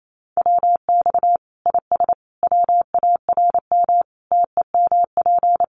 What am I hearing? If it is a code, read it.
Morse code, heard as a keyed tone, about 28 wpm, 700 Hz. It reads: WX IS WARM TEMP
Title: Common QSO Phrases 28wpm